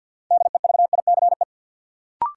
Morse code, heard as a keyed tone, about 47 wpm, 700 Hz. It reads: DEVICE
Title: Sentences from Top 1500 Words - New Word Drill - 50wpm